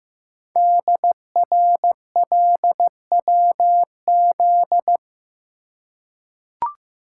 Morse code, heard as a keyed tone, about 15 wpm, 700 Hz. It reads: DRLWZ
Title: Code Groups - 5 Letters - 15wpm